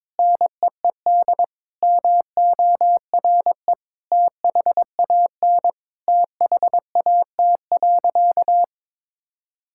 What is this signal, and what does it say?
Morse code, heard as a keyed tone, about 22 wpm, 700 Hz. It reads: NEED MORE THAN THAT.
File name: practice / 22wpm / Sentences with 2-Word Phrases - 22wpm